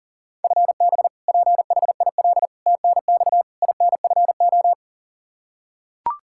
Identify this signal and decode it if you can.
Morse code, heard as a keyed tone, about 40 wpm, 700 Hz. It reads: FB PHIL TNX INFO
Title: Practice QSO - Exchange 3 - More Info - 40wpm